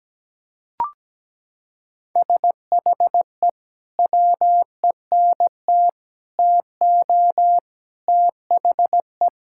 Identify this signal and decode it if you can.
Morse code, heard as a keyed tone, about 17 wpm, 700 Hz. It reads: SHE WENT TO THE
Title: Sentences from Top 1600 Words 17wpm